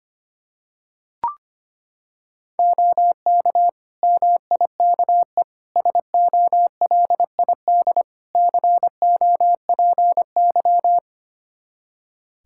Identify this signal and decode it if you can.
Morse code, heard as a keyed tone, about 25 wpm, 700 Hz. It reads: OK MIKE SOLID COPY
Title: Practice QSO - Exchange 3 - More Info - 25wpm